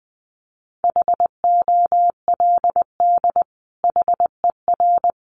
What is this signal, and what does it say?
Morse code, heard as a keyed tone, about 20 wpm, 700 Hz. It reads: HOLD HER
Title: Sentences from Top 400 Words 20wpm